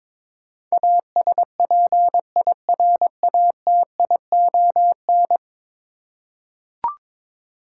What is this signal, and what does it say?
Morse code, heard as a keyed tone, about 22 wpm, 700 Hz. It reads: ASPIRATION
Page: morse-code-ninja-practice-22wpm